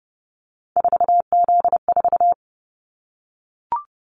Morse code, heard as a keyed tone, about 30 wpm, 700 Hz. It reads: 4Z4
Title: Callsign Prefixes for Top 50 DXCC - 30wpm